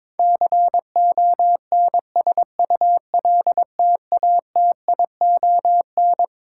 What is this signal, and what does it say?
Morse code, heard as a keyed tone, about 22 wpm, 700 Hz. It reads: CONSULTATION
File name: Twelve Letter Words - 22wpm